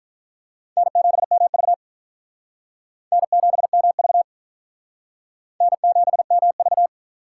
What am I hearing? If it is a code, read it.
Morse code, heard as a keyed tone, about 45 wpm, 700 Hz. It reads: N7MV N7MV N7MV